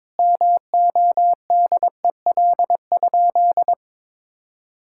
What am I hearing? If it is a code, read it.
Morse code, heard as a keyed tone, about 22 wpm, 700 Hz. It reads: MODEL?
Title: Sentences from Top 1000 Words - Review Part 5 - 22wpm